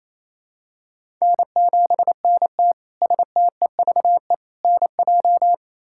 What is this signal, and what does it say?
Morse code, heard as a keyed tone, about 28 wpm, 700 Hz. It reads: N7NT STEVE NJ
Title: K1USN SST QSO Practice - 28wpm